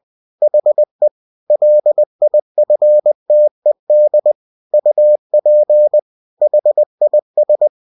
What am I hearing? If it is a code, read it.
Morse code, heard as a keyed tone, about 20 wpm, 600 Hz. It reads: HE LIFTED UP HIS